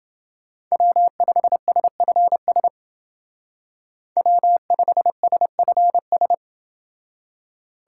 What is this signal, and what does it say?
Morse code, heard as a keyed tone, about 29 wpm, 700 Hz. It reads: W5SFS W5SFS